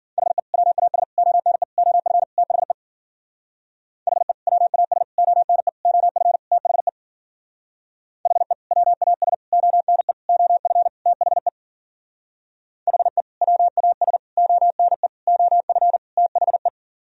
Morse code, heard as a keyed tone, about 52 wpm, 700 Hz. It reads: HE WAS ONE OF THE HE WAS ONE OF THE HE WAS ONE OF THE HE WAS ONE OF THE